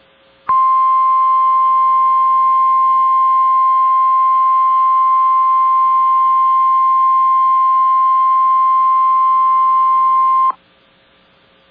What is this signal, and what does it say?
Emergency Alert System (SAME) tones (the 1050 Hz attention tone). It is heard from 0.5 to 10.6 s.